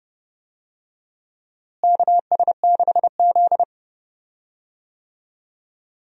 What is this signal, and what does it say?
Morse code, heard as a keyed tone, about 32 wpm, 700 Hz. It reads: KS6Z
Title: USA Call Signs - 4 Characters - Repeated 2x - 30wpm